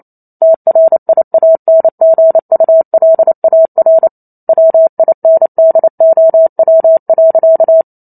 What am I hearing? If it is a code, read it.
Morse code, heard as a keyed tone, about 28 wpm, 650 Hz. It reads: TRIANGULAR WINDOW.
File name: Chapter 2 - 28 WPM